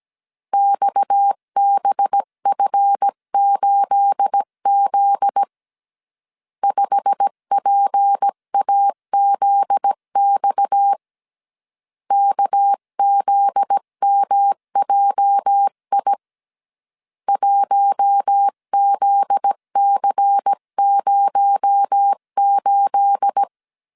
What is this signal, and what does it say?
Morse code, heard as a keyed tone, about 17 wpm, 800 Hz. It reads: XBF8Z 5PAZX KZMJI 1ZC08
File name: CW / DET / 1 9 1500C S B 16wpm 2856